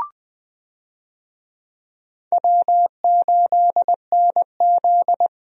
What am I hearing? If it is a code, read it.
Morse code, heard as a keyed tone, about 21 wpm, 700 Hz. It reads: W8NZ
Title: USA Call Signs - 4 Characters - Repeated 3x - 20wpm